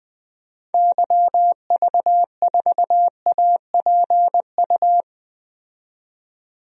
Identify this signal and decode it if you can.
Morse code, heard as a keyed tone, about 21 wpm, 700 Hz. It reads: YV4APU
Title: DXCC-100 Call Signs - 6 Characters - 20wpm